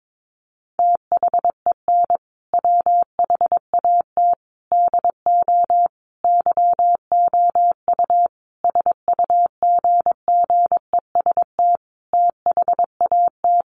Morse code, heard as a keyed tone, about 22 wpm, 700 Hz. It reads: THEN WHAT DO YOU SUGGEST THAT
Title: Sentences from Top 1300 Words 22wpm